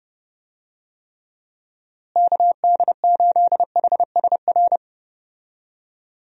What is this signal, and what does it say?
Morse code, heard as a keyed tone, about 31 wpm, 700 Hz. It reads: KD8HSR
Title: USA Call Signs - 6 Characters - Repeated 2x - 30wpm